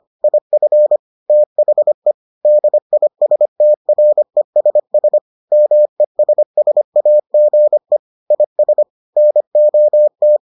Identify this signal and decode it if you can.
Morse code, heard as a keyed tone, about 25 wpm, 600 Hz. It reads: IF THE DISTRESS MESSAGE IS NOT